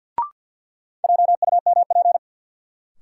Morse code, heard as a keyed tone, about 52 wpm, 700 Hz. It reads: JUMP
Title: Sentences from Top 1300 Words - New Word Drill Alt - 50wpm